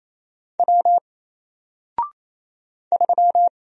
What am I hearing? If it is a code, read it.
Morse code, heard as a keyed tone, about 27 wpm, 700 Hz. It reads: W 3
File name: Single-Letter-Number-28wpm